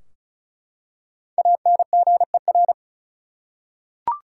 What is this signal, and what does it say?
Morse code, heard as a keyed tone, about 36 wpm, 700 Hz. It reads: ANGER
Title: Five Letter Words - 35wpm